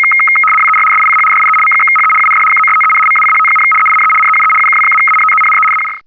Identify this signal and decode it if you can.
RTTY, about 100 baud. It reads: WER SICH DIE M/HE UND KOSTEN MACHT, EINE NEUE UKW-ANTENNE AUX
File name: digimodes / Digimodes mp3 / RTTY-100-850